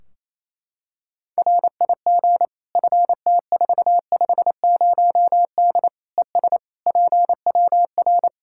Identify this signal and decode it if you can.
Morse code, heard as a keyed tone, about 28 wpm, 700 Hz. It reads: RIG FT450D ES PWR
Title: Practice QSO - Exchange 2 - Info - 28wpm